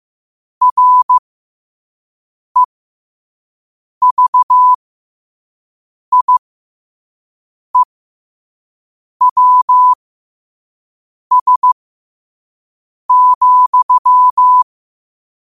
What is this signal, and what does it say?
Morse code, heard as a keyed tone, about 15 wpm, spaced as for 5 wpm, 1000 Hz. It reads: REVIEWS,